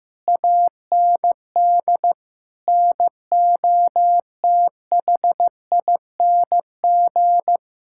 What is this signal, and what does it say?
Morse code, heard as a keyed tone, about 15 wpm, 700 Hz. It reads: AND NOTHING
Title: Sentences from Top 1000 Words - Review Part 5 - 15wpm